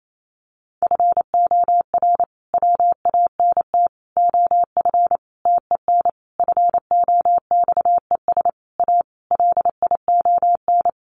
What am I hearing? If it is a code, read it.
Morse code, heard as a keyed tone, about 28 wpm, 700 Hz. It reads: FOR WANT OF TEN FOXES A LION
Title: A Sound Of Thunder 28wpm